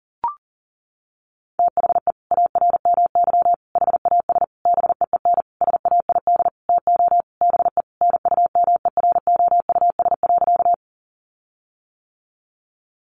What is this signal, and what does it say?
Morse code, heard as a keyed tone, about 40 wpm, 700 Hz. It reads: THE ARMY HAS BEEN SAID TO BE NUMEROUS.
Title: Sentences from Top 1400 Words 40wpm